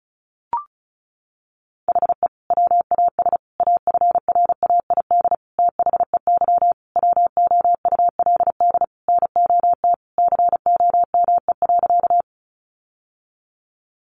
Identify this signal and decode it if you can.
Morse code, heard as a keyed tone, about 34 wpm, 700 Hz. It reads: HE WAS AFRAID THEY WOULD NOT COME.
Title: Sentences from Top 1500 Words 35wpm